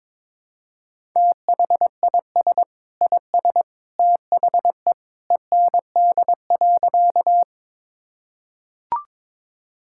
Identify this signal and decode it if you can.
Morse code, heard as a keyed tone, about 22 wpm, 700 Hz. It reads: THIS IS THE END.
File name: practice / 22wpm / Sentences with 2-Word Phrases - 22wpm